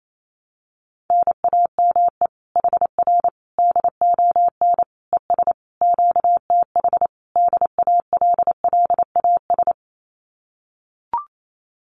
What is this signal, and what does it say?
Morse code, heard as a keyed tone, about 28 wpm, 700 Hz. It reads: NAME HR DON ES QTH DALLAS
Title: Common QSO Phrases Encore 28wpm